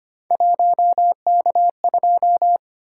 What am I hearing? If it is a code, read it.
Morse code, heard as a keyed tone, about 29 wpm, 700 Hz. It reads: 1K2
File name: Speed-Racing - Three Letter-Number Combinations - 25wpm